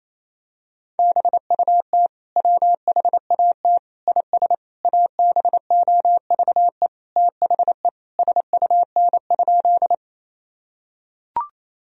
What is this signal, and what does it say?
Morse code, heard as a keyed tone, about 28 wpm, 700 Hz. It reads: BUT WHAT IS ABOVE THE SUN?
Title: Sentences from Top 1000 Words - Review Part 2 - 28wpm